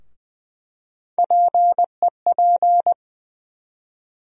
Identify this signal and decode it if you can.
Morse code, heard as a keyed tone, about 20 wpm, 700 Hz. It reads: PEP
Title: USA Call Sign Postfixes - 3 Letters 20wpm